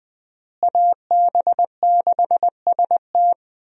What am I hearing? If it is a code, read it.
Morse code, heard as a keyed tone, about 20 wpm, 700 Hz. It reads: AB6ST
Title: USA Call Signs - 5 Characters - Repeated 3x - 20wpm